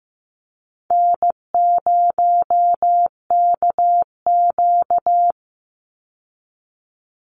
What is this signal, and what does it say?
Morse code, heard as a keyed tone, about 15 wpm, 700 Hz. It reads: N0KQ
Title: Easy Call Signs Repeated 3x - 15wpm